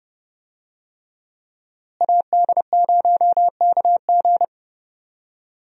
Morse code, heard as a keyed tone, about 32 wpm, 700 Hz. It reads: AD0KG
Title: USA Call Signs - 5 Characters - Repeated 3x - 30wpm